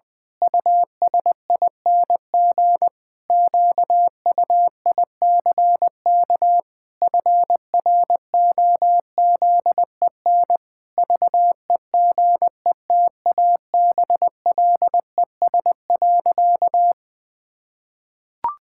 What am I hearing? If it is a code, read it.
Morse code, heard as a keyed tone, about 20 wpm, 700 Hz. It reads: USING QUICK FROZEN VEGETABLES.
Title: Pangram Challenge v2 - 20wpm